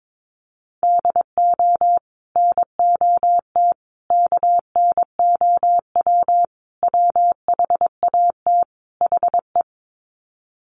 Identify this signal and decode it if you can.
Morse code, heard as a keyed tone, about 22 wpm, 700 Hz. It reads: DO NOT KNOW WHAT HE
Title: Sets of 5 Words Encore2 22wpm